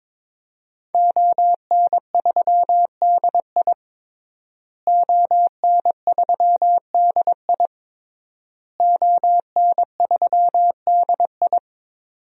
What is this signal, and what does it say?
Morse code, heard as a keyed tone, about 23 wpm, 700 Hz. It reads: ON3DI ON3DI ON3DI
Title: DXCC-100 Call Signs - 5 Characters - Repeated 3x - 22wpm